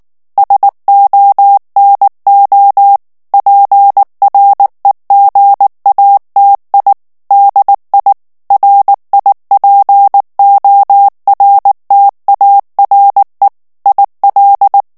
Morse code, heard as a keyed tone, about 19 wpm, 800 Hz. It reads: SONO PREGATI DI RIPORTARE IL
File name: morse mp3 / ITALIEN texte clair 2 vitesse 18 mots minute